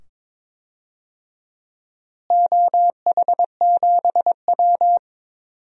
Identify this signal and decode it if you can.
Morse code, heard as a keyed tone, about 23 wpm, 700 Hz. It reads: OH7W
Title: DXCC-100 Call Signs - 4 Characters - Repeated 2x - 22wpm